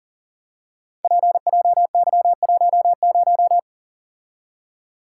Morse code, heard as a keyed tone, about 38 wpm, 700 Hz. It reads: PJY10
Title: Code Groups - 5 Letter:Numbers - 40wpm